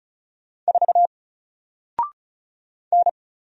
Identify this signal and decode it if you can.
Morse code, heard as a keyed tone, about 36 wpm, 700 Hz. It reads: 4 N
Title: Single Letter-Number Alt 35wpm